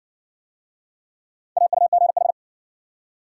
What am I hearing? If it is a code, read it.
Morse code, heard as a keyed tone, about 57 wpm, 700 Hz. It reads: AUGH